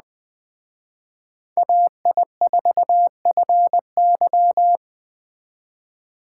Morse code, heard as a keyed tone, about 21 wpm, 700 Hz. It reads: AI4FY
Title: Speed Racing - USA 2x2 Call Signs 17wpm